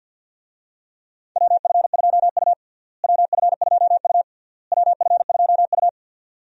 Morse code, heard as a keyed tone, about 51 wpm, 700 Hz. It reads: WF2U WF2U WF2U
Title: USA Call Signs - 4 Characters - Repeated 3x - 50wpm